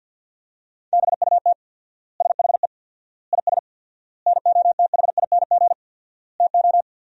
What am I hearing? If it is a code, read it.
Morse code, heard as a keyed tone, about 47 wpm, 700 Hz. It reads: BUT SHE IS NOTHING TO
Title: 4x Word-Spacing - Sentences from Top 500 Words - 50wpm